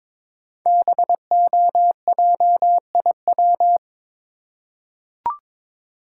Morse code, heard as a keyed tone, about 22 wpm, 700 Hz. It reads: BOJIW
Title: Code Groups - 5 Letters - 22wpm